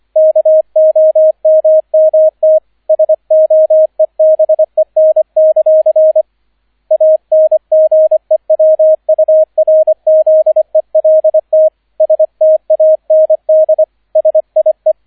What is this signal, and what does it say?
Morse code, heard as a keyed tone, about 25 wpm, 630 Hz, with radio noise. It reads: KOMMT SOEBEN; ANGEWURZELT STAND SIE